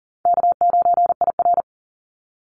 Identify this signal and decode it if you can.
Morse code, heard as a keyed tone, about 42 wpm, 700 Hz. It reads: K9IR
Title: NA SOTA Chaser Callsigns 40wpm